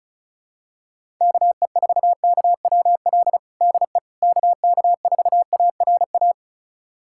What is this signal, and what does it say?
Morse code, heard as a keyed tone, about 36 wpm, 700 Hz. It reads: KE4KWL DE KK4ARA
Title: USA Callsign DE Callsign - 6 Characters - 35wpm